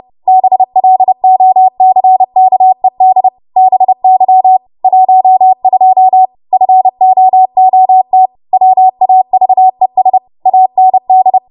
Morse code, heard as a keyed tone, about 30 wpm, 750 Hz. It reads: BLOCKED BY 12 FOOT WAVES AND
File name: Morse / 30WPM